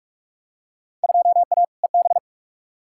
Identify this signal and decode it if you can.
Morse code, heard as a keyed tone, about 46 wpm, 700 Hz. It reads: 2A EB
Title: Field Day Exchanges - 45wpm